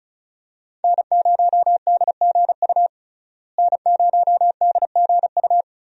Morse code, heard as a keyed tone, about 34 wpm, 700 Hz. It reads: N0DGU N0DGU
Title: USA Call Signs - 5 Characters - Repeated 3x - 35wpm